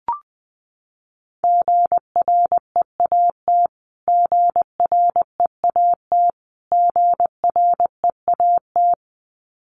Morse code, heard as a keyed tone, about 19 wpm, 700 Hz. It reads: GREAT GREAT GREAT
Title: Top 200 Words Repeated 20wpm